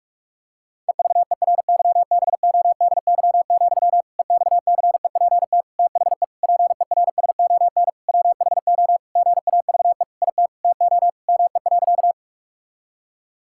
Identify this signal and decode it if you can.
Morse code, heard as a keyed tone, about 46 wpm, 700 Hz. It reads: EVERYBODY, EXCEPT THE PERSON WHO GAVE IT TO ME.
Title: Sentences from Top 1600 Words 45wpm